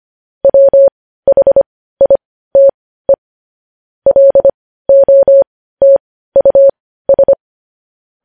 Morse code, heard as a keyed tone, about 25 wpm, spaced as for 15 wpm, 550 Hz. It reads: WHITE LOTUS
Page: News Headlines in Morse Code at 15 WPM